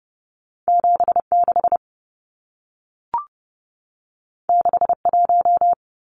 Morse code, heard as a keyed tone, about 30 wpm, 700 Hz. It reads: 76 61
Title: Numbers - 2-Digit 30wpm